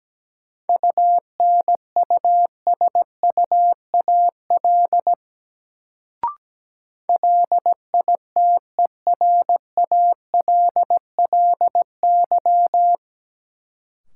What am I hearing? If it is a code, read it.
Morse code, heard as a keyed tone, about 17 wpm, 700 Hz. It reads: UNUSUAL LITERALLY
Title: Sentences from Top 1400 Words - New Word Drill - 17wpm